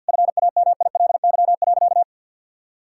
Morse code, heard as a keyed tone, about 50 wpm, 700 Hz. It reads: FAMILY.
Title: Sentences from Top 1000 Words - Review Part 6 - 50wpm